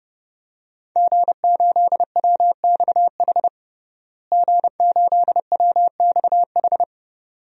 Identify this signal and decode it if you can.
Morse code, heard as a keyed tone, about 29 wpm, 700 Hz. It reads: G8WXH G8WXH
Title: DXCC-100 Call Signs - 5 Characters - Repeated 2x - 30wpm